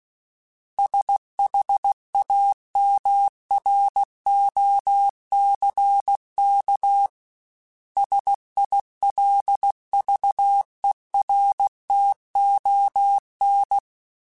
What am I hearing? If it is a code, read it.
Morse code, heard as a keyed tone, about 16 wpm, 800 Hz. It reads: SHAMROCK SILVERTON
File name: fromSMs / FromSMs mp3s / Cities@15